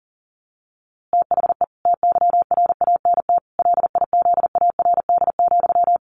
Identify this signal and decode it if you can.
Morse code, heard as a keyed tone, about 40 wpm, 700 Hz. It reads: THE TYRANT LIZARD,
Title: A Sound Of Thunder 40wpm